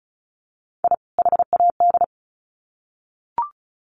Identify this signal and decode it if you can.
Morse code, heard as a keyed tone, about 34 wpm, 700 Hz. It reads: I HAD
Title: Sets of 2 Words 35wpm